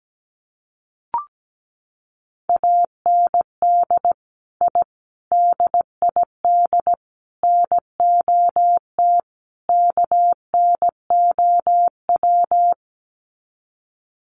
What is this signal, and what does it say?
Morse code, heard as a keyed tone, about 17 wpm, 700 Hz. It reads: AND I DID NOT KNOW
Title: Sets of 5 Words Encore 17wpm